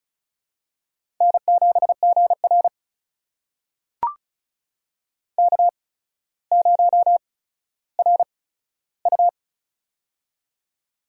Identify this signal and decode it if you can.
Morse code, heard as a keyed tone, about 34 wpm, spaced as for 15 wpm, 700 Hz. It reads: N7GR K0RU
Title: USA Call Signs - 4 Characters - 8x Character Spacing - 35wpm